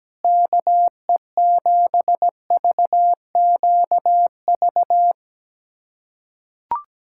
Morse code, heard as a keyed tone, about 17 wpm, 700 Hz. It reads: KE7VQV
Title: USA Callsign DE Callsign - 6 Characters - 17wpm